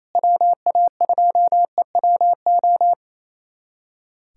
Morse code, heard as a keyed tone, about 28 wpm, 700 Hz. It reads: WA2EWO
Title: USA Callsign DE Callsign 28wpm